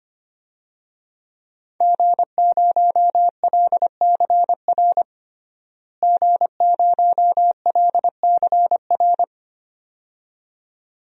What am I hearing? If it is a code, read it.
Morse code, heard as a keyed tone, about 25 wpm, 700 Hz. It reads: G0LCR G0LCR
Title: DXCC-100 Call Signs - 5 Characters - Repeated 2x - 25wpm